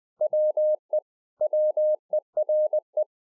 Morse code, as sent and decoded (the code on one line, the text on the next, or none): .-- . / .-- . .-. .
WE WERE